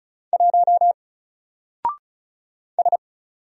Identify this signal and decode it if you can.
Morse code, heard as a keyed tone, about 35 wpm, 700 Hz. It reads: 1 S
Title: Single Character (Letter:Number:Punctuation:Prosign) 35wpm